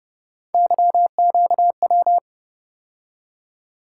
Morse code, heard as a keyed tone, about 32 wpm, 700 Hz. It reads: YQW